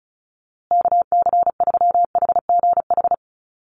Morse code, heard as a keyed tone, about 34 wpm, 700 Hz. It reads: KC3HGH
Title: USA Call Signs Repeated 2x 35wpm